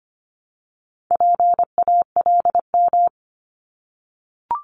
Morse code, heard as a keyed tone, about 25 wpm, 700 Hz. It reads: PALM
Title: Multi-letter Phonemes 25wpm